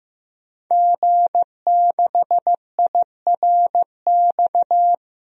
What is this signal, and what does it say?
Morse code, heard as a keyed tone, about 15 wpm, 700 Hz. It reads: G6IRX
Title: DXCC-100 Call Signs - 5 Characters - 15wpm